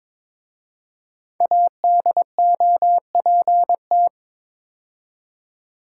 Morse code, heard as a keyed tone, about 22 wpm, 700 Hz. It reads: ADOPT
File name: 5 Letter Words - First 3 Letters Given - 22wpm